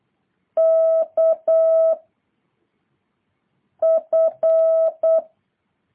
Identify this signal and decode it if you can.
Morse code, heard as a keyed tone, about 8 wpm, 650 Hz. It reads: K F